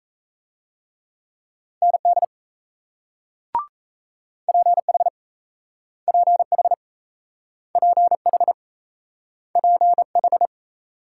Morse code, heard as a keyed tone, about 38 wpm, 700 Hz. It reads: ND PH PH PH PH